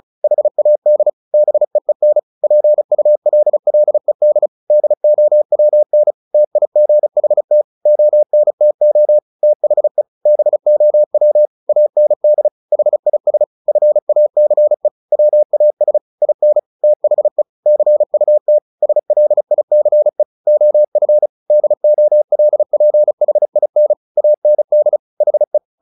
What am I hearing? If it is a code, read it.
Morse code, heard as a keyed tone, about 34 wpm, 600 Hz. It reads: HAD BEEN PULLED DOWN TIGHT ONTO THE BOW AND HIS FACE WAS IN THE CUT SLICE OF DOLPHIN AND HE
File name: the old man and the sea 2 35wpm